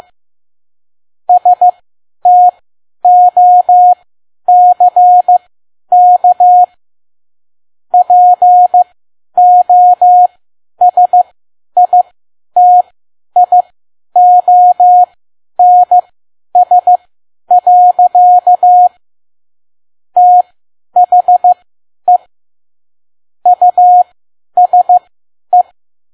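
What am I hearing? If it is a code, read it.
Morse code, heard as a keyed tone, about 15 wpm, spaced as for 10 wpm, 720 Hz. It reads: STOCK POSITIONS. THE USE
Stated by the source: Morse code practice at 10 WPM.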